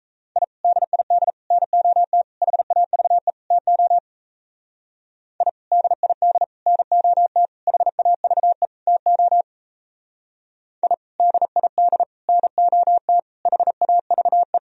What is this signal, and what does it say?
Morse code, heard as a keyed tone, about 36 wpm, 700 Hz. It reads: I DID NOT HAVE TO I DID NOT HAVE TO I DID NOT HAVE